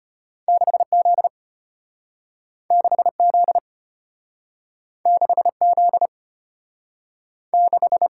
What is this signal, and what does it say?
Morse code, heard as a keyed tone, about 32 wpm, 700 Hz. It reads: 6Z 6Z 6Z 6